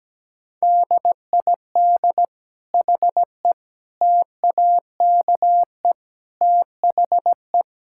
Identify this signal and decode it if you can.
Morse code, heard as a keyed tone, about 17 wpm, 700 Hz. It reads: DID HE TAKE THE
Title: Sentences from Top 1000 Words - Review Part 6 - 17wpm